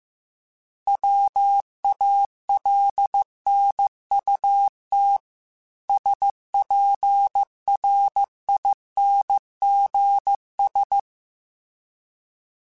Morse code, heard as a keyed tone, about 15 wpm, 800 Hz. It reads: WALNUT SPRINGS